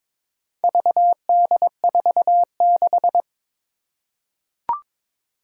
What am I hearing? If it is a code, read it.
Morse code, heard as a keyed tone, about 22 wpm, 700 Hz. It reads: VD46
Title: Four Challenging Letter-Number Combinations 22wpm